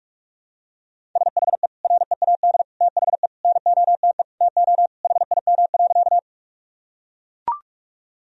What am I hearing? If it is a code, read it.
Morse code, heard as a keyed tone, about 41 wpm, 700 Hz. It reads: SHE READ THE NOTE TO HIM.
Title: Sentences from Top 1000 Words - Review Part 2 - 45wpm